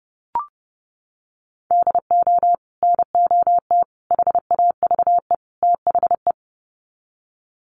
Morse code, heard as a keyed tone, about 30 wpm, 700 Hz. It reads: DO NOT HAVE THE
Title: Sets of 4 Words 30wpm